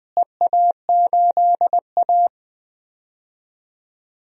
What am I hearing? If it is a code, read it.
Morse code, heard as a keyed tone, about 20 wpm, 700 Hz. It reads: EA8A
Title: DXCC-100 Call Signs - 4 Characters - Repeated 3x - 20wpm